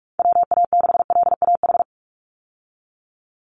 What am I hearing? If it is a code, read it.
Morse code, heard as a keyed tone, about 44 wpm, 700 Hz. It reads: WA6LAH
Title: USA Call Signs 45wpm